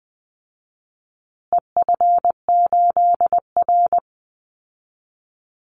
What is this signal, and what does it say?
Morse code, heard as a keyed tone, about 20 wpm, 700 Hz. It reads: EF8R
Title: Call Signs Repeated 2x - 20wpm